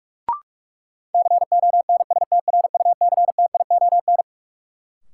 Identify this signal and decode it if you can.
Morse code, heard as a keyed tone, about 41 wpm, 700 Hz. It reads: CONSTRUCTION
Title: Sentences from Top 1600 Words - New Word Drill - 45wpm